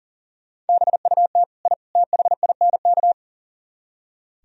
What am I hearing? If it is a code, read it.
Morse code, heard as a keyed tone, about 37 wpm, 700 Hz. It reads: BUT I THINK
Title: Speed-Racing - Sets of 3 Words 40wpm